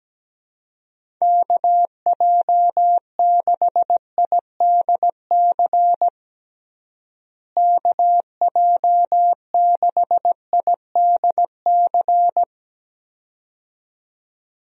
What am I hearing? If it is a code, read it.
Morse code, heard as a keyed tone, about 17 wpm, 700 Hz. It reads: KJ6IDC KJ6IDC
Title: USA Call Signs - 6 Characters - Repeated 2x - 17wpm